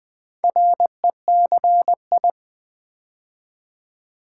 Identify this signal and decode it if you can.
Morse code, heard as a keyed tone, about 20 wpm, 700 Hz. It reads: RECI
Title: Build A Word - 2 Letters At A Time - 6-Letter Words - 20wpm